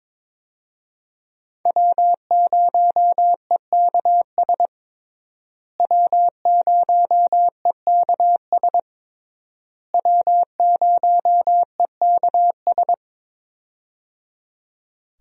Morse code, heard as a keyed tone, about 22 wpm, 700 Hz. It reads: W0EKS W0EKS W0EKS
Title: USA Call Signs - 5 Characters - Repeated 3x - 22wpm